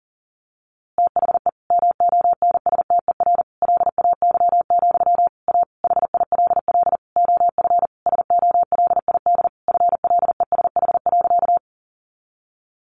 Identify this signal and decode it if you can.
Morse code, heard as a keyed tone, about 37 wpm, 700 Hz. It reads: THE MONSTER LAY, A HILL OF SOLID FLESH.
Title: A Sound Of Thunder 40wpm